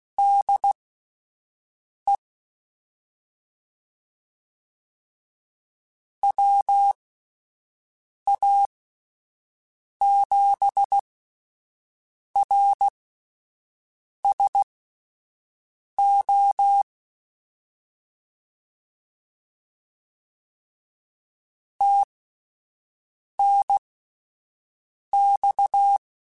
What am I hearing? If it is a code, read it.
Morse code, heard as a keyed tone, about 16 wpm, spaced as for 5 wpm, 800 Hz. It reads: DE WA7RSO TNX
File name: (spaced) / QSOs Spaced-mp3s / kc7haa @10 sp